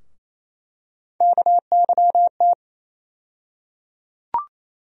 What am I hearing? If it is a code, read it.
Morse code, heard as a keyed tone, about 28 wpm, 700 Hz. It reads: KYT
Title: USA Call Sign Postfixes - 3 Letters 28wpm